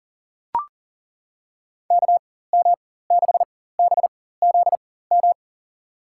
Morse code, heard as a keyed tone, about 38 wpm, spaced as for 18 wpm, 700 Hz. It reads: KM6BZM
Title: USA Call Signs - 6 Characters - 4x Character Spacing - 40wpm